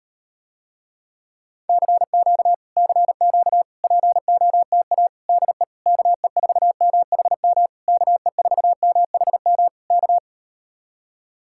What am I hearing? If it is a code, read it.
Morse code, heard as a keyed tone, about 38 wpm, 700 Hz. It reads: CQ CQ POTA DE KE4MHM KE4MHM K